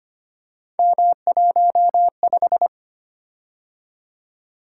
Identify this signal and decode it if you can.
Morse code, heard as a keyed tone, about 25 wpm, 700 Hz. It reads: M15
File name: Three Letter-Number Combinations - 25wpm